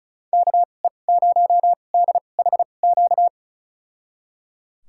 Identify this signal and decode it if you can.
Morse code, heard as a keyed tone, about 34 wpm, spaced as for 25 wpm, 700 Hz. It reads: KE0DHQ
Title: USA Call Signs - 6 Characters - 2x Character Spacing - 35wpm